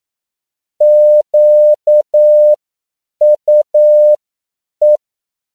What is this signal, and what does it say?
Morse code, heard as a keyed tone, about 9 wpm, spaced as for 7 wpm, 600 Hz. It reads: QUE
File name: CW / 10PPM